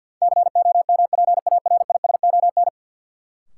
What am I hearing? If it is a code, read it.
Morse code, heard as a keyed tone, about 53 wpm, 700 Hz. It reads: COMPARISON
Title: Sentences from Top 1600 Words - New Word Drill Alt - 50wpm